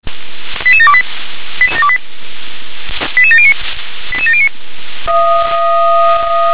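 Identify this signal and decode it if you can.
Five-tone selective call (ZVEI), heard as quick runs of five tones, tones of about 70 ms. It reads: 88728 88709 88709
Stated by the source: Received on SDR as a ZVEI five-tone selective call, address 88126.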